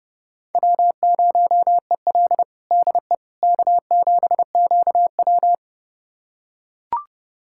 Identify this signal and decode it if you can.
Morse code, heard as a keyed tone, about 32 wpm, 700 Hz. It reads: W0EL DE K7QW
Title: USA Callsign DE Callsign - 4 Characters - 30wpm